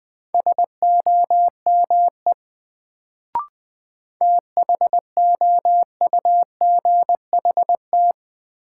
Morse code, heard as a keyed tone, about 20 wpm, 700 Hz. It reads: SOME THOUGHT
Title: Sets of 2 Words - New Word Drill - 20wpm